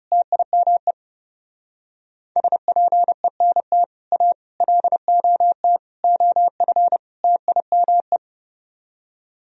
Morse code, heard as a keyed tone, about 35 wpm, 700 Hz. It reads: TIME SPENT A LOT OF TIME
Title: Speed-Racing - Sets of 5 Words 30wpm